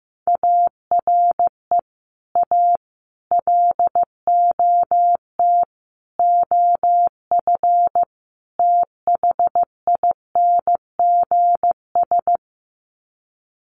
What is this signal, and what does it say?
Morse code, heard as a keyed tone, about 15 wpm, 700 Hz. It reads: ARE A LOT OF THINGS
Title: Sets of 5 Words Encore2 15wpm